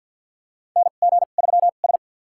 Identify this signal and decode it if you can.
Morse code, heard as a keyed tone, about 46 wpm, spaced as for 35 wpm, 700 Hz. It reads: NG3S
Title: USA Call Signs - 2x Character Spacing - 50wpm